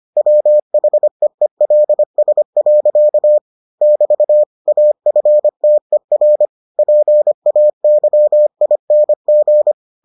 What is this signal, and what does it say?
Morse code, heard as a keyed tone, about 25 wpm, 600 Hz. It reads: WHEELS. <BT> AFTER PAYING